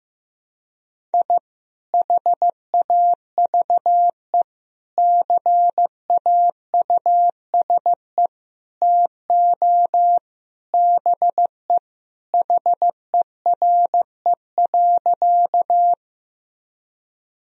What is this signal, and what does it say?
Morse code, heard as a keyed tone, about 15 wpm, 700 Hz. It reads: I HAVE CAUSE TO BE HERE.
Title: Sentences from Top 1000 Words - Review Part 1 - 15wpm